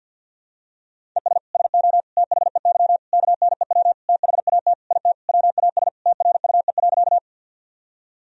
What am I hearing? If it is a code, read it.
Morse code, heard as a keyed tone, about 51 wpm, 700 Hz. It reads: ES SO THEY KNEW THAT IT WAS TRUE.
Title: Sentences from Top 400 Words and Abbreviations 50wpm